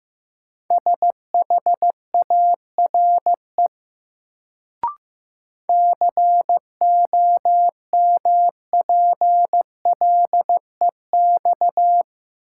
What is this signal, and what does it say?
Morse code, heard as a keyed tone, about 15 wpm, 700 Hz. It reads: SHARE COMPLEX
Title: Sentences from Top 900 Words - New Word Drill - 15wpm